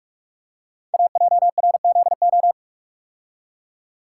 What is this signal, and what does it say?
Morse code, heard as a keyed tone, about 44 wpm, 700 Hz. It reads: AJRZO
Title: Code Groups - 5 Letter:Numbers - 45wpm